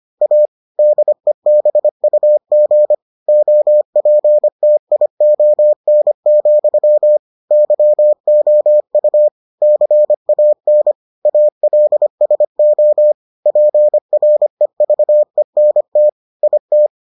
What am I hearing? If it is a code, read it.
Morse code, heard as a keyed tone, about 25 wpm, 600 Hz. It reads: A DEBUG OPTION, YOU CAN ALSO PREVENT IT